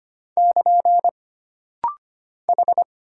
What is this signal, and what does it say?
Morse code, heard as a keyed tone, about 25 wpm, 700 Hz. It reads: <KN> H
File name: Single Character (Letter:Number:Punctuation:Prosign) - Rapid-Fire - 25wpm